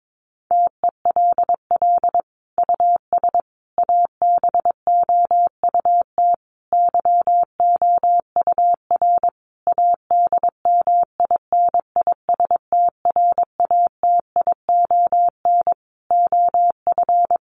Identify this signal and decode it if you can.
Morse code, heard as a keyed tone, about 22 wpm, 700 Hz. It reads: TELL US ABOUT YOUR ADMINISTRATION OF
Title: Sentences from Top 1300 Words 22wpm